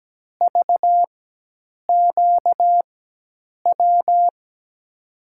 Morse code, heard as a keyed tone, about 17 wpm, spaced as for 8 wpm, 700 Hz. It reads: VQW
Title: USA Call Signs - 6 Characters - 4x Character Spacing - 17wpm